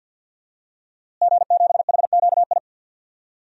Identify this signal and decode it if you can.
Morse code, heard as a keyed tone, about 48 wpm, 700 Hz. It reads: G7HQI
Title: DXCC-100 Call Signs - 5 Characters - Repeated 2x - 50wpm